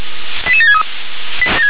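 Five-tone selective call (ZVEI), heard as quick runs of five tones, tones of about 70 ms. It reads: 88763
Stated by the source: ZVEI five-tone selective call 88559 received on an SDR.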